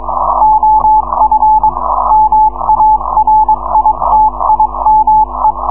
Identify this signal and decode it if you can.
Morse code, heard as a keyed tone, about 22 wpm, 860 Hz. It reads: OF MARITIME